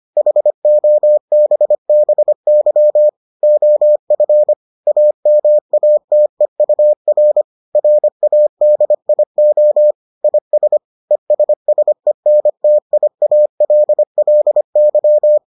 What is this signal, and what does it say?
Morse code, heard as a keyed tone, about 25 wpm, 600 Hz. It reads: HOBBY OF AMATEUR RADIO IS ESSENTIALLY